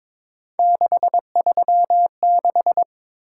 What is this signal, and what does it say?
Morse code, heard as a keyed tone, about 22 wpm, 700 Hz. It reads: 636
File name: practice / 22wpm / Three Letter-Number Combinations - 22wpm